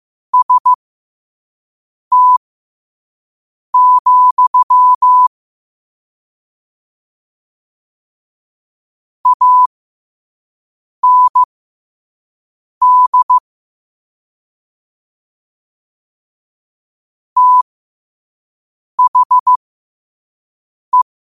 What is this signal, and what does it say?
Morse code, heard as a keyed tone, about 15 wpm, spaced as for 5 wpm, 1000 Hz. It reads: ST, AND THE
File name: qotdfiles / 5wpm